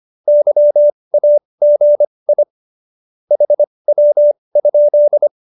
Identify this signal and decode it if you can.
Morse code, heard as a keyed tone, about 25 wpm, spaced as for 20 wpm, 590 Hz. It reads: YAGI HW?
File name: qso208b 20